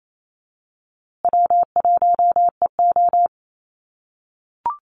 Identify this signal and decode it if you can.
Morse code, heard as a keyed tone, about 28 wpm, 700 Hz. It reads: W1EO
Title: USA Call Signs Repeated 3x 28wpm